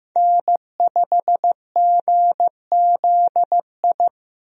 Morse code, heard as a keyed tone, about 15 wpm, 700 Hz. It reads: N5GZI
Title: USA Call Signs - 5 Characters - 15wpm